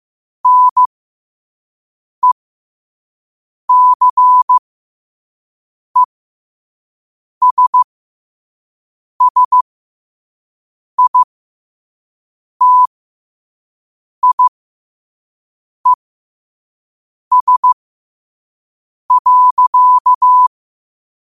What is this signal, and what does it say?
Morse code, heard as a keyed tone, about 15 wpm, spaced as for 5 wpm, 1000 Hz. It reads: NECESSITIES.